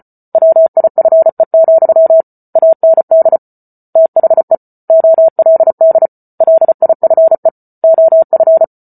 Morse code, heard as a keyed tone, about 34 wpm, 650 Hz. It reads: WIFE, AND THE OLD LIFE OF
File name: Book Two / 08-Chapter 8 - 34 WPM